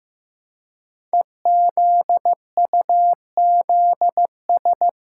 Morse code, heard as a keyed tone, about 15 wpm, 700 Hz. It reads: EZUZS
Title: Code Groups - 5 Letters - 15wpm